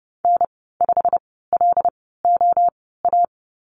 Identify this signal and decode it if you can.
Morse code, heard as a keyed tone, about 30 wpm, spaced as for 17 wpm, 700 Hz. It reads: N5LOA
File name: Lesson 19g - Letter w - Call Signs 30wpm